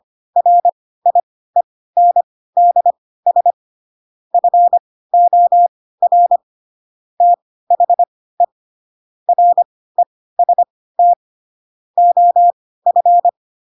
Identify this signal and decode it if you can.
Morse code, heard as a keyed tone, about 25 wpm, spaced as for 16 wpm, 700 Hz. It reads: RIENDS FOR THE REST OF
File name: Pinocchio / Pinocchio ch2 10 25 ik1hge